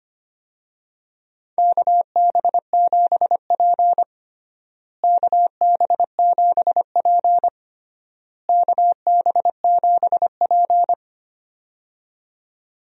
Morse code, heard as a keyed tone, about 25 wpm, 700 Hz. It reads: KB7P KB7P KB7P
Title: USA Call Signs - 4 Characters - Repeated 3x - 25wpm